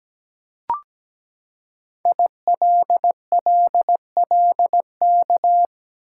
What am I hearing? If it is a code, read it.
Morse code, heard as a keyed tone, about 17 wpm, 700 Hz. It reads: ILLLK
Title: Code Groups - 5 Letters - 17wpm